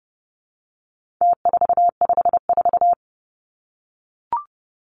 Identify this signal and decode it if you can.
Morse code, heard as a keyed tone, about 30 wpm, 700 Hz. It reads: T454
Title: Numbers - Contest Style - 4-Digit 30wpm